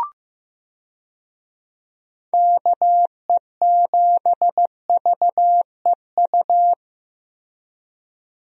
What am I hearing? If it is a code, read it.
Morse code, heard as a keyed tone, about 15 wpm, 700 Hz. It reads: KE7VEU
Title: USA Call Signs - 6 Characters - Repeated 2x - 15wpm